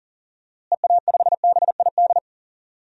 Morse code, heard as a keyed tone, about 40 wpm, 700 Hz. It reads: EA5BID
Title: DXCC-100 Call Signs - 6 Characters - Repeated 2x - 40wpm